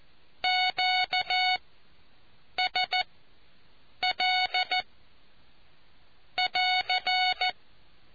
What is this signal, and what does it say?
Morse code, heard as a keyed tone, about 14 wpm, spaced as for 7 wpm, 740 Hz. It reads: QSL <AR>